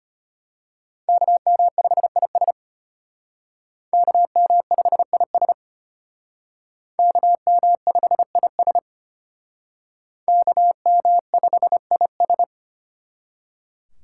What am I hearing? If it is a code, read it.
Morse code, heard as a keyed tone, about 31 wpm, 700 Hz. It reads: KM5IS KM5IS KM5IS KM5IS
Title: Speed Racing - USA 2x2 Call Signs 25wpm